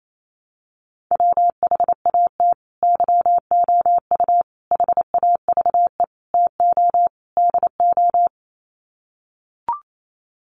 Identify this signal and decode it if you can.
Morse code, heard as a keyed tone, about 28 wpm, 700 Hz. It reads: WHAT YOU HAVE TO DO
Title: Sets of 5 Words Encore 28wpm